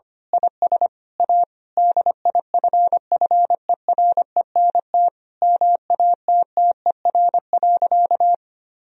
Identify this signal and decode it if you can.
Morse code, heard as a keyed tone, about 25 wpm, 700 Hz. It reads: IS A DIFFERENT MATTER.
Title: Sentences from Top 1000 Words - Review Part 3 - 25wpm